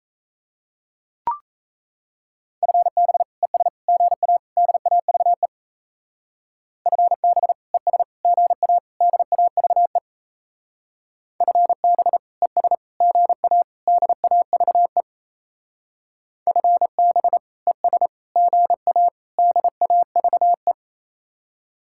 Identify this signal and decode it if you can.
Morse code, heard as a keyed tone, about 34 wpm, 700 Hz. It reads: FB ES GA DAVE FB ES GA DAVE FB ES GA DAVE FB ES GA DAVE
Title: Speed-Racing Common QSO Phrases - 28wpm